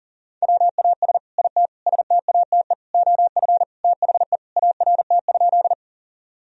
Morse code, heard as a keyed tone, about 40 wpm, 700 Hz. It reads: WAS IT STATE OF THE ART?
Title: Sentences from Top 1000 Words - Review Part 3 - 40wpm